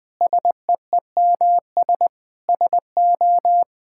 Morse code, heard as a keyed tone, about 20 wpm, 700 Hz. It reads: SEEMS SO
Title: Sentences with 5-Word Phrases - Spoken Beginning - 20wpm